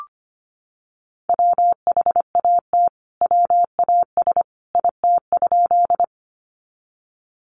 Morse code, heard as a keyed tone, about 25 wpm, 700 Hz. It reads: WHAT WAS IT?
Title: Sentences from Top 100 Words 25wpm